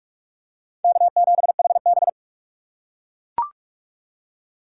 Morse code, heard as a keyed tone, about 44 wpm, 700 Hz. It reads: K7HB